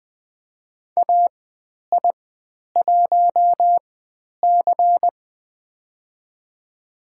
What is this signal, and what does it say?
Morse code, heard as a keyed tone, about 20 wpm, spaced as for 10 wpm, 700 Hz. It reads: AI1C
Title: 095 Lesson 16g - Letter c - Call Signs 20wpm